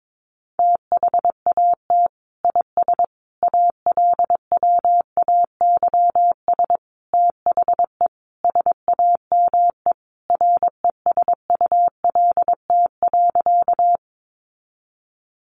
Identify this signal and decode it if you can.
Morse code, heard as a keyed tone, about 22 wpm, 700 Hz. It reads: THAT IS ALWAYS THE SAME RESULT.
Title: Sentences from Top 700 Words 22wpm